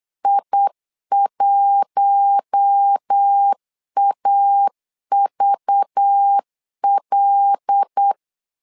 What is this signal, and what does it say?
Morse code, heard as a keyed tone, about 8 wpm, 800 Hz. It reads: I1AVL
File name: LEZIONI / 5 LBVC / 5 LBV 8wpm 2837